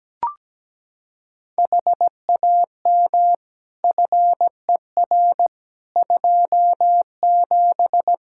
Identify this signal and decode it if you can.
Morse code, heard as a keyed tone, about 17 wpm, 700 Hz. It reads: HAM FER 27
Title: Practice QSO - Exchange 3 - More Info - 17wpm